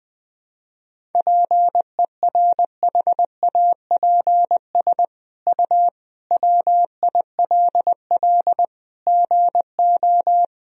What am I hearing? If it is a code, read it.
Morse code, heard as a keyed tone, about 20 wpm, 700 Hz. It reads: PERHAPS U WILL GO
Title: Sentences from Top 500 Words and Abbreviations 20wpm